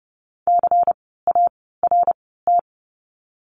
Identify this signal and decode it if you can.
Morse code, heard as a keyed tone, about 30 wpm, spaced as for 17 wpm, 700 Hz. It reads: CART
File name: Lesson 16f - Letter c - Words 30wpm